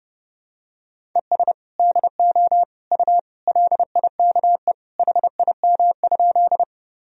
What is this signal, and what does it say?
Morse code, heard as a keyed tone, about 32 wpm, 700 Hz. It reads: ES DO U LIKE HIM?
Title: Sentences from Top 100 Words and Abbreviations 30wpm